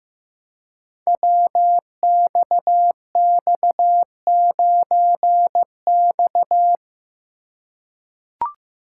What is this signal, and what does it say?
Morse code, heard as a keyed tone, about 15 wpm, 700 Hz. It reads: WXX9X
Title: Code Groups - 5 Letter:Numbers - 15wpm